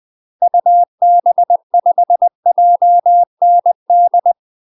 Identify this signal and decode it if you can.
Morse code, heard as a keyed tone, about 20 wpm, 700 Hz. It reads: UB5JND
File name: UK MASTER QSOs TRAINING / 20WPM / UK MASTER QSOs 0051